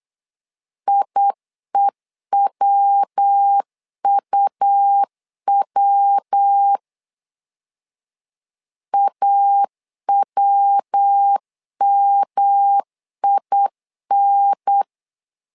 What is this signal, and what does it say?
Morse code, heard as a keyed tone, about 8 wpm, 800 Hz. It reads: IEWUW AWMIN
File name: LEZIONI / 3 MWRO / 3 MW 8wpm 2501